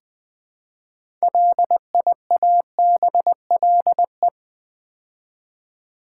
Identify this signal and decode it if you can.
Morse code, heard as a keyed tone, about 20 wpm, 700 Hz. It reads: LIABLE
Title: Build A Word - 2 Letters At A Time - 6-Letter Words - 20wpm